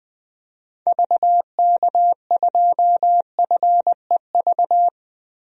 Morse code, heard as a keyed tone, about 21 wpm, 700 Hz. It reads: VK2FEV
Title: DXCC-100 Call Signs - 6 Characters - Repeated 2x - 20wpm